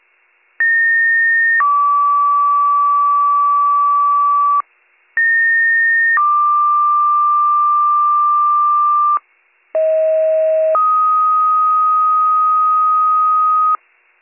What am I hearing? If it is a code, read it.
Two-tone page, heard as a one-second tone, then a three-second tone, about 1850 then 1150 Hz, about 1850 then 1180 Hz, about 640 then 1250 Hz. It is heard from 0.6 to 13.8 s.